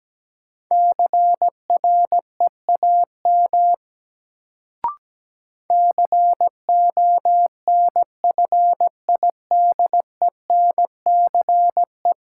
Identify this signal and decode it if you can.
Morse code, heard as a keyed tone, about 17 wpm, 700 Hz. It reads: CREAM CONFIDENCE
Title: Sentences from Top 1500 Words - New Word Drill Alt - 17wpm